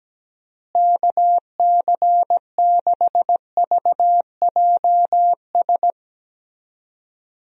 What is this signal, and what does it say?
Morse code, heard as a keyed tone, about 17 wpm, 700 Hz. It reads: KC6VJS
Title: USA Call Signs - 6 Characters - Repeated 2x - 17wpm